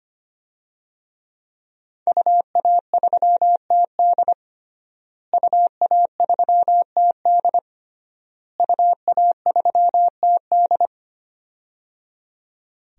Morse code, heard as a keyed tone, about 25 wpm, 700 Hz. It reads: UA3TD UA3TD UA3TD
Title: DXCC-100 Call Signs - 5 Characters - Repeated 3x - 25wpm